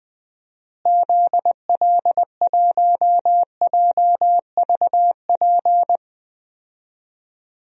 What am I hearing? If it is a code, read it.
Morse code, heard as a keyed tone, about 20 wpm, 700 Hz. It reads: ZL1JVP
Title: DXCC-100 Call Signs - 6 Characters - Repeated 3x - 20wpm